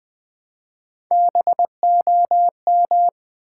Morse code, heard as a keyed tone, about 20 wpm, 700 Hz. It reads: BOM
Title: Build A Word - 1 Letter At A Time - 4-Letter Words - 20wpm